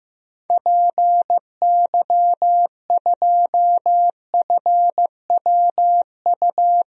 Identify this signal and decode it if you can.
Morse code, heard as a keyed tone, about 15 wpm, 700 Hz. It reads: PY2FWU
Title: DXCC-100 Call Signs - 6 Characters - Repeated 2x - 15wpm